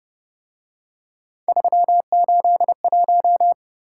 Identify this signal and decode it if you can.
Morse code, heard as a keyed tone, about 29 wpm, 700 Hz. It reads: 381